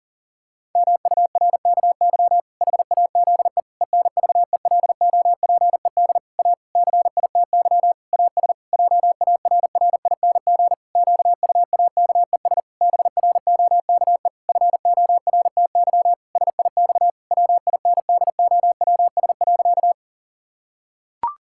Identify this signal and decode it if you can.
Morse code, heard as a keyed tone, about 40 wpm, 700 Hz. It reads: MURKY HAZE ENVELOPED A CITY AS JARRING QUAKES BROKE FORTY SIX WINDOWS.
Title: Pangram Challenge v2 - 40wpm